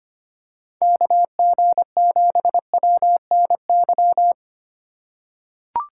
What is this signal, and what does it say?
Morse code, heard as a keyed tone, about 25 wpm, 700 Hz. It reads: KG7WNY
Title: USA Call Signs - 6 Characters - 3x Character Spacing - 25wpm